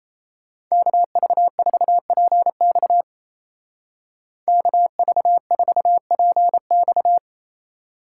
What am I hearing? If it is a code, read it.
Morse code, heard as a keyed tone, about 35 wpm, 700 Hz. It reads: KV4PX KV4PX
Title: Speed Racing - USA 2x2 Call Signs 28wpm